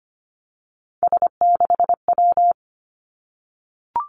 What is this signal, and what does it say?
Morse code, heard as a keyed tone, about 25 wpm, 700 Hz. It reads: S6W
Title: Three Challenging Letter-Number Combinations 25wpm